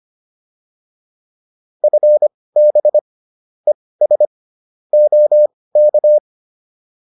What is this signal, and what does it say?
Morse code, heard as a keyed tone, about 25 wpm, spaced as for 18 wpm, 600 Hz. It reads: FB ES OK